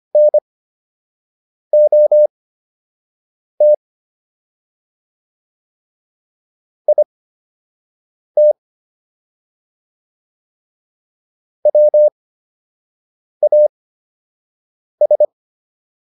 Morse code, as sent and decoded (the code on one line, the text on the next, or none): -. --- - / .. - / .-- .- ...
NOT IT WAS